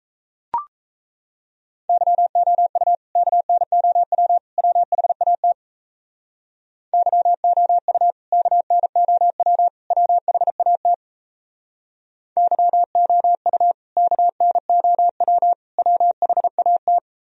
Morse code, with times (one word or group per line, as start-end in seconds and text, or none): 1.9-3.0 YOU
3.2-4.4 KNOW
4.6-5.5 WHAT
6.9-8.1 YOU
8.3-9.7 KNOW
9.9-10.9 WHAT
12.4-13.7 YOU
14.0-15.5 KNOW
15.8-17.0 WHAT